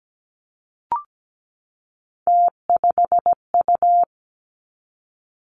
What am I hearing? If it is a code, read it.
Morse code, heard as a keyed tone, about 17 wpm, 700 Hz. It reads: T5U
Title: Three Challenging Letter-Number Combinations 17wpm